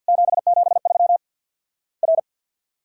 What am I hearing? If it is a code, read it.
Morse code, heard as a keyed tone, about 50 wpm, 700 Hz. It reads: 773 R
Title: Worked All Europe DX Contest - QTC Exchange with Repeats - 50wpm